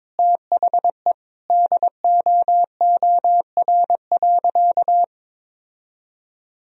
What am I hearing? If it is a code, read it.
Morse code, heard as a keyed tone, about 22 wpm, 700 Hz. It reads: THE DOOR.
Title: Sentences from Top 1000 Words - Review Part 1 - 22wpm